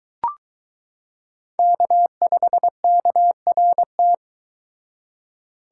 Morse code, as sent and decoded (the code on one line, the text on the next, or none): -.- ..... -.- .-. -
K5KRT